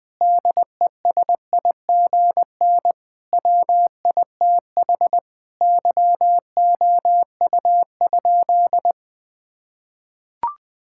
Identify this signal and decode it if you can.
Morse code, heard as a keyed tone, about 20 wpm, 700 Hz. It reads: DESIGN WITH YOU?
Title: Sentences from Top 1000 Words - Review Part 5 - 20wpm